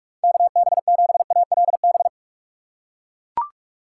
Morse code, heard as a keyed tone, about 44 wpm, 700 Hz. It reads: KB7ALB
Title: USA Call Signs - 6 Characters - 6x Character Spacing - 45wpm